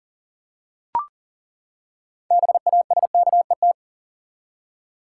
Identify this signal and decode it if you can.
Morse code, heard as a keyed tone, about 40 wpm, 700 Hz. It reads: BASKET
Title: Six Letter Words - 40wpm